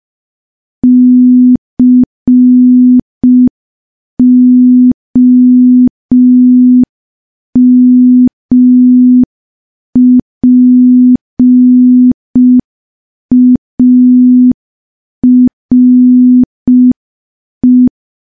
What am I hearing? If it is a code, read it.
Morse code, heard as a keyed tone, about 5 wpm, 250 Hz. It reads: COMPARE
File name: morse / morse-training-05WPM